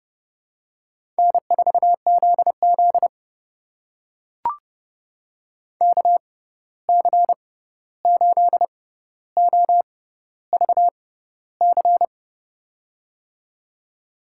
Morse code, heard as a keyed tone, about 30 wpm, spaced as for 14 wpm, 700 Hz. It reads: N4ZZ KC8OVC
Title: USA Call Signs - 6x Character Spacing - 30wpm